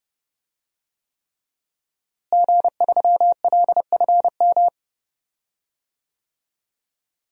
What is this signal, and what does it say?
Morse code, heard as a keyed tone, about 30 wpm, 700 Hz. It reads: G3LFM
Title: DXCC-100 Call Signs - 5 Characters - Repeated 3x - 30wpm